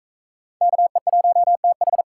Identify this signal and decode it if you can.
Morse code, heard as a keyed tone, about 44 wpm, 700 Hz. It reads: KE1TH